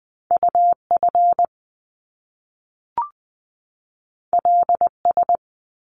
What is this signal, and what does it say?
Morse code, heard as a keyed tone, about 20 wpm, 700 Hz. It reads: UF LS
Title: Top Two Letter Combinations in Words 20wpm